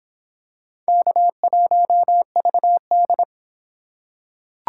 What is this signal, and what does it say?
Morse code, heard as a keyed tone, about 23 wpm, 700 Hz. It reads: K1VD